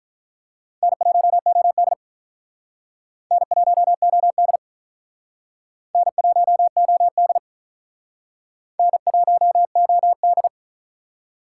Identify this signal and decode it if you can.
Morse code, heard as a keyed tone, about 47 wpm, 700 Hz. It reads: N1OD N1OD N1OD N1OD